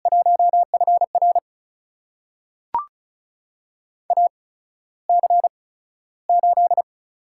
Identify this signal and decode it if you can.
Morse code, heard as a keyed tone, about 36 wpm, spaced as for 15 wpm, 700 Hz. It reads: 1FR AC8
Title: USA Call Signs - 5 Characters - 8x Character Spacing - 35wpm